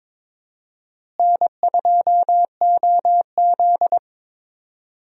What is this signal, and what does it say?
Morse code, heard as a keyed tone, about 22 wpm, 700 Hz. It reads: N2OZ
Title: USA Call Signs - 4 Characters - Repeated 2x - 22wpm